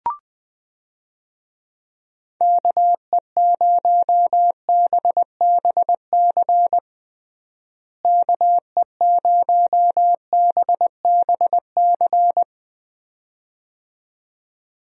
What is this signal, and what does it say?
Morse code, heard as a keyed tone, about 21 wpm, 700 Hz. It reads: KE0BBC KE0BBC
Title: USA Call Signs - 6 Characters - Repeated 2x - 20wpm